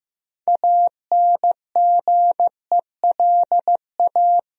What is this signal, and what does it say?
Morse code, heard as a keyed tone, about 15 wpm, 700 Hz. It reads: ANGELA
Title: USA Common Names - 15wpm